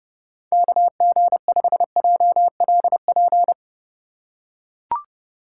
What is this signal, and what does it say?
Morse code, heard as a keyed tone, about 30 wpm, 700 Hz. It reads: KG5JLP
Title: USA Call Signs - 6 Characters - 3x Character Spacing - 30wpm